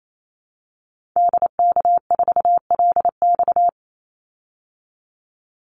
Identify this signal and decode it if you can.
Morse code, heard as a keyed tone, about 28 wpm, 700 Hz. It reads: DK4LX
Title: Call Signs - 28wpm